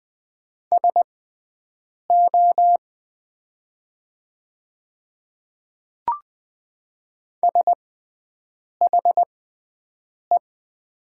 Morse code, heard as a keyed tone, about 21 wpm, spaced as for 7 wpm, 700 Hz. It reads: SO SHE
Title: 6x Character Spacing - Top 100 Words - 20wpm